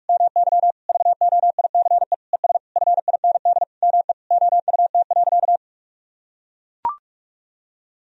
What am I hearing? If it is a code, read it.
Morse code, heard as a keyed tone, about 41 wpm, 690 Hz. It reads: MY VOICE ES FIND ME OUT.
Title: Sentences from Top 500 Words and Abbreviations 45wpm